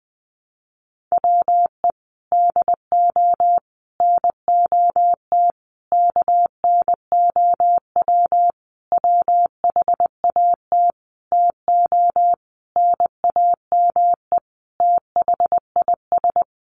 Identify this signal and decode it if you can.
Morse code, heard as a keyed tone, about 19 wpm, 700 Hz. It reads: WE DO NOT KNOW WHAT TO NAME THIS
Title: Sentences from Top 1500 Words 20wpm